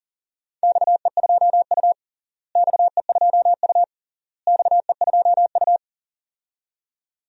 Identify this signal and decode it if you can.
Morse code, heard as a keyed tone, about 40 wpm, 700 Hz. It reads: XE2U XE2U XE2U
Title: DXCC-100 Call Signs - 4 Characters - Repeated 3x - 40wpm